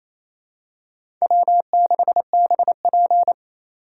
Morse code, heard as a keyed tone, about 28 wpm, 700 Hz. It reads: W6BP
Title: USA Call Signs - 4 Characters - 3x Character Spacing - 28wpm